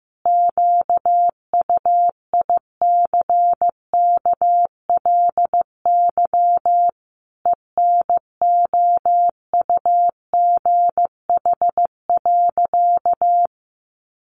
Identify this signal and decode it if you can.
Morse code, heard as a keyed tone, about 14 wpm, 700 Hz. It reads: QUICKLY ENOUGH.
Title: Sentences from Top 700 Words 15wpm